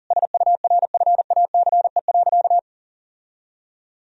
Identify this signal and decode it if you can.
Morse code, heard as a keyed tone, about 40 wpm, 700 Hz. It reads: SURFACE.
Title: Sentences from Top 1000 Words - Review Part 3 - 40wpm